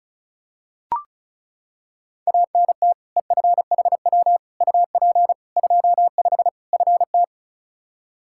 Morse code, heard as a keyed tone, about 34 wpm, 700 Hz. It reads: ANT EFHW UP 25 FT
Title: Practice QSO - Exchange 2 - Info - 35wpm